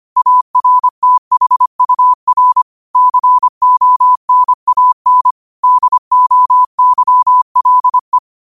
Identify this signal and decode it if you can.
Morse code, heard as a keyed tone, about 25 wpm, 1000 Hz. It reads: ARTHUR CONAN DOYLE
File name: qotdfiles / 25wpm